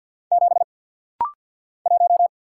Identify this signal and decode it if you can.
Morse code, heard as a keyed tone, about 50 wpm, 700 Hz. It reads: 7 1
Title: Single Letter-Number - Rapid-Fire 50wpm